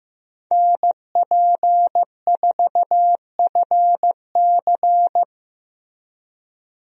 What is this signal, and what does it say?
Morse code, heard as a keyed tone, about 15 wpm, 700 Hz. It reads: NP4FC
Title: USA Call Signs - 5 Characters - Repeated 3x - 15wpm